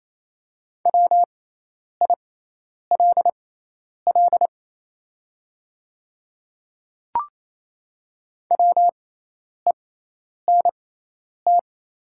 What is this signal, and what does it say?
Morse code, heard as a keyed tone, about 28 wpm, spaced as for 10 wpm, 700 Hz. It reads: WILL WENT
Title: 6x Character Spacing - Top 500 Words - 28wpm